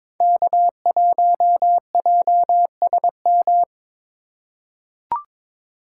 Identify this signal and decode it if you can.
Morse code, heard as a keyed tone, about 22 wpm, 700 Hz. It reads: K1JSM
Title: USA Call Signs - 5 Characters - 3x Character Spacing - 22wpm